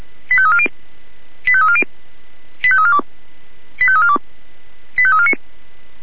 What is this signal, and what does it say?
Five-tone selective call (ZVEI), heard as quick runs of five tones, tones of about 70 ms. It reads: 85340 85340 85342 85342 85349
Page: /496/SDR/ZVEI/85342/2023.03.12/